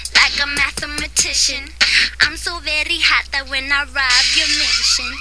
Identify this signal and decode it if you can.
Emergency Alert System (SAME) tones, heard as harsh buzzing bursts. It is heard from 0.5 to 4.2 s.